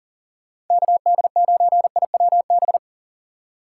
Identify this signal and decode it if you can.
Morse code, heard as a keyed tone, about 52 wpm, 700 Hz. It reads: KD9IWB